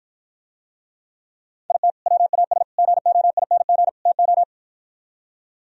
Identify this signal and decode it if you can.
Morse code, heard as a keyed tone, about 52 wpm, 700 Hz. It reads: IT WAS GOING TO